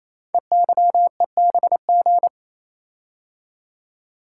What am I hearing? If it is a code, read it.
Morse code, heard as a keyed tone, about 29 wpm, 700 Hz. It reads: EYEBG
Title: Code Groups - 5 Letters - 28wpm